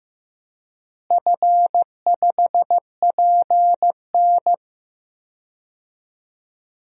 Morse code, heard as a keyed tone, about 15 wpm, 700 Hz. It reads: F5PN
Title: DXCC-100 Call Signs - 4 Characters - Repeated 2x - 15wpm